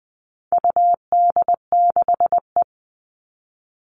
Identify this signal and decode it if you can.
Morse code, heard as a keyed tone, about 20 wpm, 700 Hz. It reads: UD6E
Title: All Cut Numbers - 4-Digit 20wpm